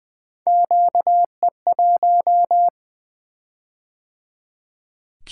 Morse code, heard as a keyed tone, about 20 wpm, 700 Hz. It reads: QE1
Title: Three Letter-Number Combinations - 20wpm